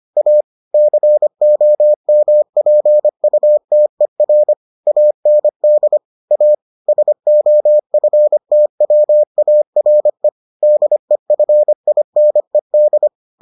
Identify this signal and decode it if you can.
Morse code, heard as a keyed tone, about 25 wpm, 600 Hz. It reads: A COMPUTER AND A SOFTWARE DEFINED